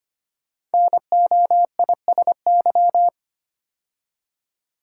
Morse code, heard as a keyed tone, about 25 wpm, 700 Hz. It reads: NOISY
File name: 5 Letter Words - First 3 Letters Given - 25wpm